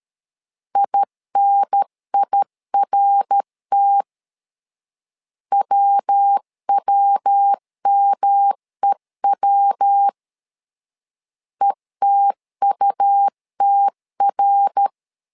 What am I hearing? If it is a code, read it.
Morse code, heard as a keyed tone, about 13 wpm, 800 Hz. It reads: INIRT WWMEW ETUTR
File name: LEZIONI / 3 MWRO / 3 MWR 12wpm 1711